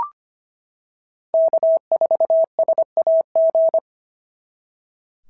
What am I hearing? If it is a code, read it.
Morse code, heard as a keyed tone, about 25 wpm, 690 Hz. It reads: K4SAG
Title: SOTA QSO Practice 25wpm